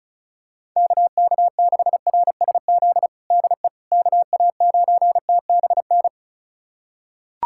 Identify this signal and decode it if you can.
Morse code, heard as a keyed tone, about 34 wpm, 700 Hz. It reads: KK6RSZ DE KA9TBN
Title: USA Callsign DE Callsign 35wpm